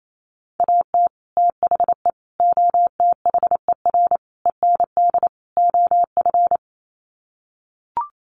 Morse code, heard as a keyed tone, about 28 wpm, 700 Hz. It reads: AT THE OTHER END OF
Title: Sets of 5 Words 28wpm